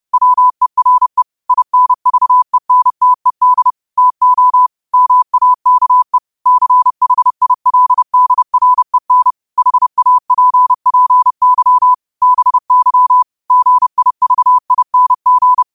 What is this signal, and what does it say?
Morse code, heard as a keyed tone, about 30 wpm, 1000 Hz. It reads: WERE INVENTED TO MAKE CHILDREN HAPPY BY GIVING
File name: qotdfiles / 30wpm